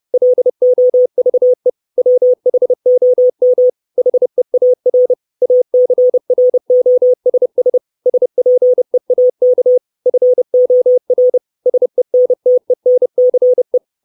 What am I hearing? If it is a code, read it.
Morse code, heard as a keyed tone, about 30 wpm, 500 Hz. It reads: LOVE WHOM HEAR ACROSS SPEAK FOR SENTENCE
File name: 632 Words 30WPM